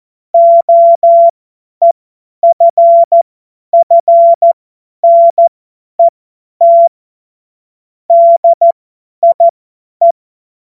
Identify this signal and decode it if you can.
Morse code, heard as a keyed tone, about 14 wpm, spaced as for 10 wpm, 680 Hz. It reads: OEFFNET DIE
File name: Morsekurs / Witze / Witz 11. Januar 2022, 50 F70